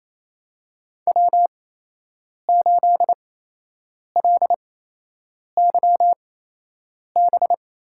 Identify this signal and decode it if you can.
Morse code, heard as a keyed tone, about 28 wpm, spaced as for 8 wpm, 700 Hz. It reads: W8LYB
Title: USA Call Signs - 5 Characters - 8x Character Spacing - 28wpm